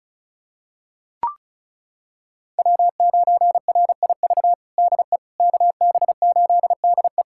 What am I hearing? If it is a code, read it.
Morse code, heard as a keyed tone, about 36 wpm, 700 Hz. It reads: W9RIV DE KB8DE
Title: USA Callsign DE Callsign - 5 Characters - 35wpm